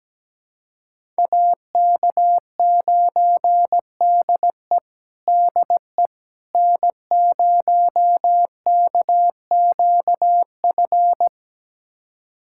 Morse code, heard as a keyed tone, about 17 wpm, 700 Hz. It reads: AK9DE DE N0KQF
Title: USA Callsign DE Callsign - 5 Characters - 17wpm